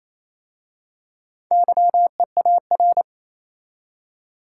Morse code, heard as a keyed tone, about 27 wpm, 690 Hz. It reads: YEAR
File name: Easy Words - Speed-Racing with Hints - 22wpm